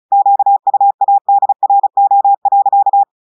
Morse code, heard as a keyed tone, about 35 wpm, 800 Hz. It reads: QUADRO.
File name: audios / portugais / portugais texte en clair 1 vitesse 35 mots minutes